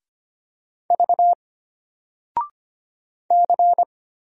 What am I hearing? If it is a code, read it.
Morse code, heard as a keyed tone, about 25 wpm, 700 Hz. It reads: V C
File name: Single-Letter-Number-25wpm